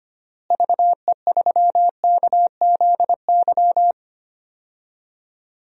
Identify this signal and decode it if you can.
Morse code, heard as a keyed tone, about 25 wpm, 700 Hz. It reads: VE3KZY
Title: DXCC-100 Call Signs - 6 Characters - Repeated 2x - 25wpm